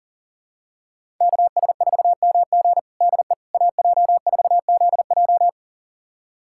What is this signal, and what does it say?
Morse code, heard as a keyed tone, about 40 wpm, 700 Hz. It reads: KS4MG DE AJ4ZJ
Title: USA Callsign DE Callsign - 5 Characters - 40wpm